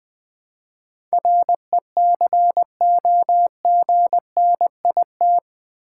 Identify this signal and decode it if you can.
Morse code, heard as a keyed tone, about 20 wpm, 700 Hz. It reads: RECOGNIT
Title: Build A Word - 4 Letters At A Time - 11-Letter Words - 20wpm